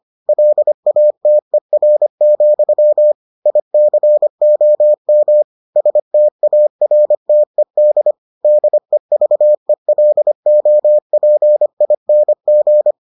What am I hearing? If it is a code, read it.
Morse code, heard as a keyed tone, about 25 wpm, 600 Hz. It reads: LATER, ICOM STARTED DEVELOPING